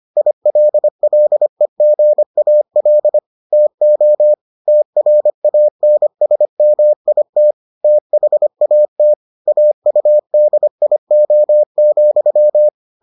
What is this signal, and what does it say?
Morse code, heard as a keyed tone, about 25 wpm, 600 Hz. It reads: ILLEGAL TO TRANSMIT THAT AUDIO,